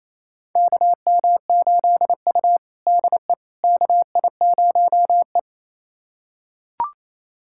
Morse code, heard as a keyed tone, about 28 wpm, 700 Hz. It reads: KM8U DE KI0E
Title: USA Callsign DE Callsign - 4 Characters - 28wpm